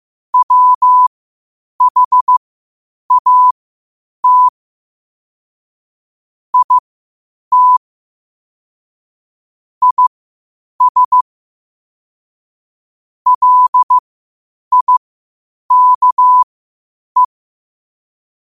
Morse code, heard as a keyed tone, about 15 wpm, spaced as for 8 wpm, 1000 Hz. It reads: WHAT IT IS LIKE
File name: qotdfiles / 8wpm